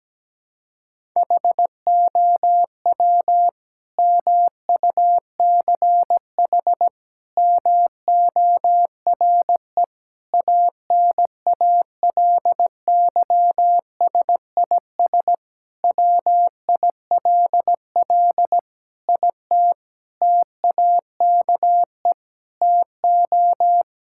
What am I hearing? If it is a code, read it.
Morse code, heard as a keyed tone, about 17 wpm, 700 Hz. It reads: HOW MUCH MORE ANALYSIS WILL IT TAKE TO
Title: Sentences from Top 1000 Words - Review Part 5 - 17wpm